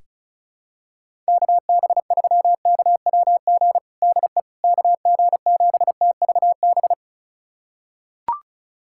Morse code, heard as a keyed tone, about 34 wpm, 700 Hz. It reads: KB3KWG DE KG7TVB
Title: USA Callsign DE Callsign 35wpm